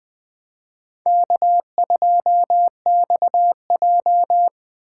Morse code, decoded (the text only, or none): K2XJ